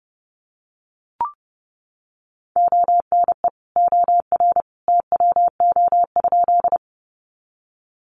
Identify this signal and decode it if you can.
Morse code, heard as a keyed tone, about 30 wpm, 700 Hz. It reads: ONE OR TWO?
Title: Sentences from Top 100 Words 30wpm